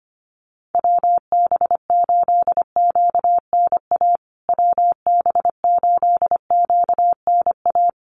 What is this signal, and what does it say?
Morse code, heard as a keyed tone, about 25 wpm, 700 Hz. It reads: WB8QNA WB8QNA
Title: USA Call Signs Repeated 3x 25wpm